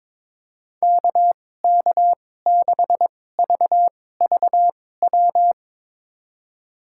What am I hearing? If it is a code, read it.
Morse code, heard as a keyed tone, about 22 wpm, spaced as for 16 wpm, 700 Hz. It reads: KK6VVW
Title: USA Call Signs - 2x Character Spacing - 22wpm